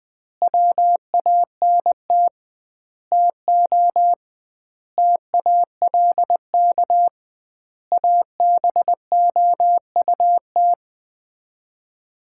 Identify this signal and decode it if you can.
Morse code, heard as a keyed tone, about 20 wpm, 700 Hz. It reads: WANT TO TALK ABOUT
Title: 2x Word-Spacing - Sets of 5 Words - 20wpm